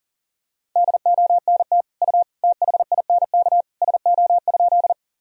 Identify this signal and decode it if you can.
Morse code, heard as a keyed tone, about 40 wpm, 700 Hz. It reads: DONT U THINK SO?
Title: Sentences from Top 300 Words and Abbreviations 40wpm